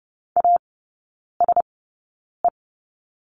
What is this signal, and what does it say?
Morse code, heard as a keyed tone, about 30 wpm, spaced as for 9 wpm, 700 Hz. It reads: ASE
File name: course / 30:10wpm / 087 Lesson 15e - Number 5 - Three Characters 30wpm